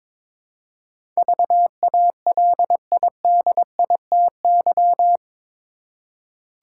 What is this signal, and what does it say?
Morse code, heard as a keyed tone, about 22 wpm, 700 Hz. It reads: VALIDITY
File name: practice / 22wpm / Eight Letter Words - 22wpm